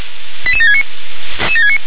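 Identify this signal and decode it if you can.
Five-tone selective call (ZVEI), heard as quick runs of five tones, tones of about 70 ms. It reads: 88769
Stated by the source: ZVEI five-tone call received by SDR.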